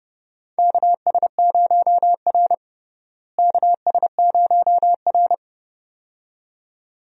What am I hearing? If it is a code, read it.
Morse code, heard as a keyed tone, about 30 wpm, 700 Hz. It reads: KS0R KS0R
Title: USA Call Signs - 4 Characters - Repeated 2x - 30wpm